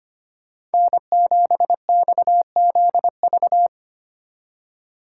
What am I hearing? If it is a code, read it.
Morse code, heard as a keyed tone, about 30 wpm, 700 Hz. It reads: N7XZV
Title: Speed Racing - USA 1x3 Call Signs 25wpm